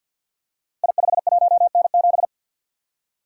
Morse code, heard as a keyed tone, about 51 wpm, 700 Hz. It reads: I51N7
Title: Code Groups - 5 Letter:Numbers - 50wpm